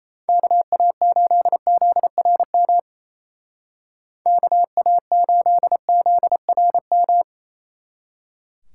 Morse code, heard as a keyed tone, about 37 wpm, 700 Hz. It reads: KA8ZRM KA8ZRM